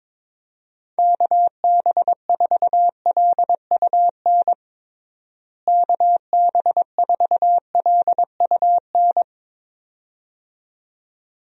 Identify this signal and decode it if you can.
Morse code, heard as a keyed tone, about 22 wpm, 700 Hz. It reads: KB4LUN KB4LUN
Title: USA Call Signs - 6 Characters - Repeated 2x - 22wpm